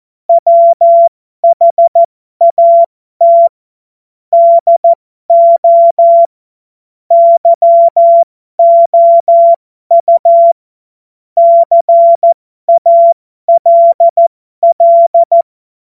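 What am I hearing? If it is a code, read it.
Morse code, heard as a keyed tone, about 14 wpm, spaced as for 12 wpm, 680 Hz. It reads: WHAT DO YOU CALL
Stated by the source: Morse code joke.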